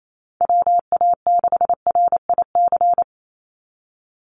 Morse code, heard as a keyed tone, about 28 wpm, 700 Hz. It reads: WA6RIC
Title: NA SOTA Chaser Callsigns 28wpm